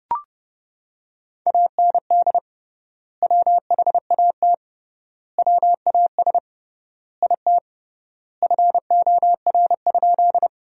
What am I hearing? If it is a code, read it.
Morse code, heard as a keyed tone, about 30 wpm, 700 Hz. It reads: AND WHAT WAS IT FOR?
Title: 3x Word-Spacing - Sentences from Top 100 Words - 30wpm